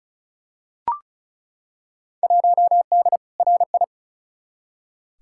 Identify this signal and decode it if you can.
Morse code, heard as a keyed tone, about 34 wpm, 700 Hz. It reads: 1D RI
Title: Field Day Exchanges - 35wpm